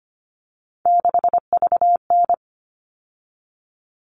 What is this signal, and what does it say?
Morse code, heard as a keyed tone, about 25 wpm, 700 Hz. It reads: 6VN
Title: All Cut Numbers - 3-Digit 25wpm